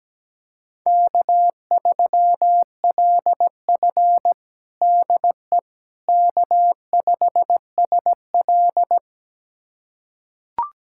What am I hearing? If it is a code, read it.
Morse code, heard as a keyed tone, about 17 wpm, 700 Hz. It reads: K3LF DE K5SL
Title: USA Callsign DE Callsign - 4 Characters - 17wpm